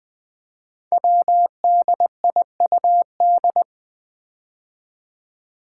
Morse code, heard as a keyed tone, about 20 wpm, 700 Hz. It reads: WDIUD
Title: Code Groups - 5 Letter:Numbers - 20wpm